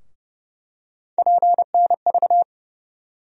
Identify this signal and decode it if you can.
Morse code, heard as a keyed tone, about 30 wpm, 700 Hz. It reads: PNV
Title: USA Call Sign Postfixes - 3 Letters 30wpm